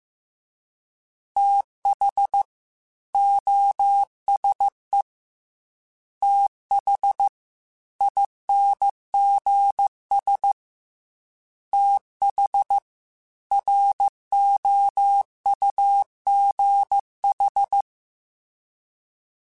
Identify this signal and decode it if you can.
Morse code, heard as a keyed tone, about 15 wpm, 800 Hz. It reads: TH OSE TH INGS TH ROUGH